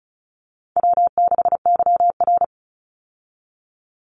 Morse code, heard as a keyed tone, about 34 wpm, 700 Hz. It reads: W6YR
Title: Easy Call Signs Repeated 2x - 35wpm